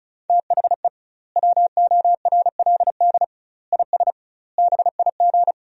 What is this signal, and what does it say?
Morse code, heard as a keyed tone, about 36 wpm, 700 Hz. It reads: THE WORLD IS BIG
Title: 2x Word-Spacing - Sentences from Top 400 Words - 35wpm